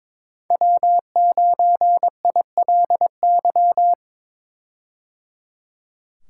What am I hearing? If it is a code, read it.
Morse code, heard as a keyed tone, about 21 wpm, 700 Hz. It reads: W9ILY
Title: K1USN SST QSO Practice - 22wpm